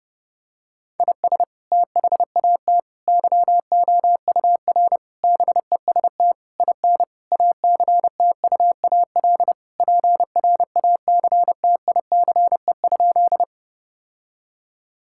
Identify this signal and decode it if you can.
Morse code, heard as a keyed tone, about 30 wpm, 700 Hz. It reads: IS THAT YOUR BEST IN ACTUAL PRACTICE?
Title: Sentences from Top 1000 Words - Review Part 5 - 30wpm